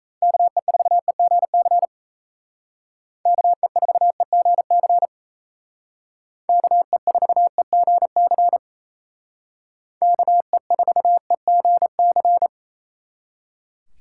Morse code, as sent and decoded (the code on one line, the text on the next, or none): -.- . ....- . --. -.-. / -.- . ....- . --. -.-. / -.- . ....- . --. -.-. / -.- . ....- . --. -.-.
KE4EGC KE4EGC KE4EGC KE4EGC